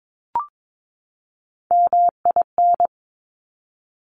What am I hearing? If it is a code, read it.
Morse code, heard as a keyed tone, about 22 wpm, 700 Hz. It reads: MIN
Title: Top Three Letter Combinations in Words 22wpm